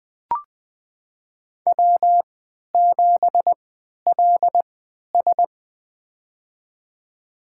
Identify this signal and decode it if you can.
Morse code, heard as a keyed tone, about 20 wpm, spaced as for 11 wpm, 700 Hz. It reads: W7LS
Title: USA Call Signs - 4 Characters - 3x Character Spacing - 20wpm